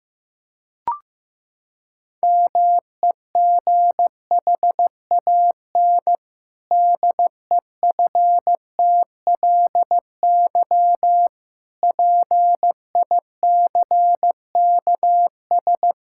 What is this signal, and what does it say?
Morse code, heard as a keyed tone, about 15 wpm, 700 Hz. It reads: MEGHAN DEFTLY PICKS
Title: Pangram Challenge v2 - 15wpm